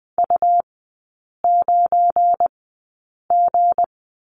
Morse code, heard as a keyed tone, about 20 wpm, spaced as for 8 wpm, 690 Hz. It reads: U9G
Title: 172 Lesson 28e - Number 9 - Three Characters 20wpm